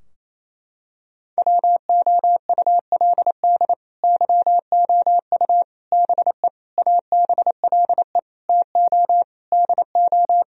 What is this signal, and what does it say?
Morse code, heard as a keyed tone, about 28 wpm, 700 Hz. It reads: WOULD YOU BE ABLE TO DO
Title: Sentences from Top 1000 Words - Review Part 2 - 28wpm